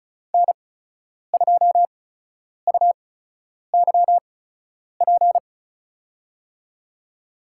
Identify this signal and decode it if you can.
Morse code, heard as a keyed tone, about 36 wpm, spaced as for 10 wpm, 700 Hz. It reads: N2UYP
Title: USA Call Signs - 8x Character Spacing - 35wpm